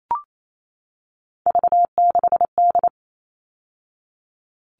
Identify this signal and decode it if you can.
Morse code, heard as a keyed tone, about 28 wpm, 700 Hz. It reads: V6D
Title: All Cut Numbers - 3-Digit 28wpm